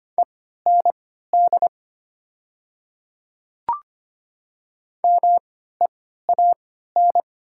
Morse code, heard as a keyed tone, about 25 wpm, spaced as for 14 wpm, 700 Hz. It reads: END MEAN
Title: 3x Character Spacing - Top 200 Words - 25wpm